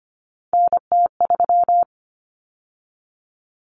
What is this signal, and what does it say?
Morse code, heard as a keyed tone, about 25 wpm, 700 Hz. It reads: NT3
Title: Common Cut Numbers - 3-Digit 25wpm